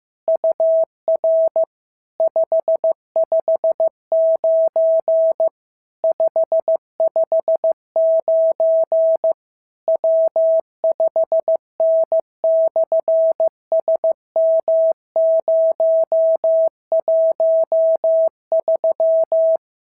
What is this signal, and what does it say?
Morse code, heard as a keyed tone, about 15 wpm, 650 Hz. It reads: UR 559 559 W5N/SM013
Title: SOTA QSO Practice 15wpm